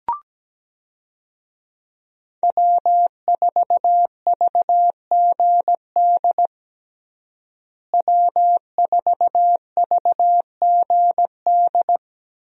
Morse code, heard as a keyed tone, about 17 wpm, 700 Hz. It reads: W4VGD W4VGD
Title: USA Call Signs - 5 Characters - Repeated 3x - 17wpm